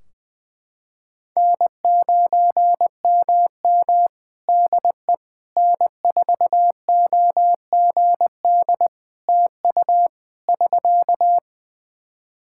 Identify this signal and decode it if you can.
Morse code, heard as a keyed tone, about 20 wpm, 700 Hz. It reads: N9MM DE N4OGD TU <SK>
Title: K1USN SST QSO Practice 20wpm